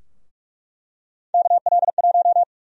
Morse code, heard as a keyed tone, about 45 wpm, 700 Hz. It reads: KL1
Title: USA Call Sign Prefixes - Repeated 2x - 45wpm